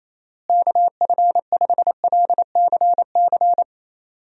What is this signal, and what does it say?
Morse code, heard as a keyed tone, about 28 wpm, 700 Hz. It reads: KF5LCC
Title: USA Call Signs - 6 Characters - Repeated 2x - 28wpm